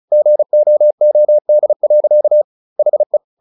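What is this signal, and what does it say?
Morse code, heard as a keyed tone, about 35 wpm, 600 Hz. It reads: GOOD. HE
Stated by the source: Morse practice at 35 wpm from CW med Gnister, sending Ken Follett's novel Never in English.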